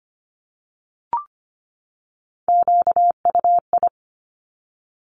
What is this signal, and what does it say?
Morse code, heard as a keyed tone, about 25 wpm, 700 Hz. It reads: QUI
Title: Top Three Letter Combinations in Words 25wpm